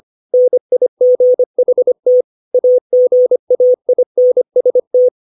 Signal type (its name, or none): Morse code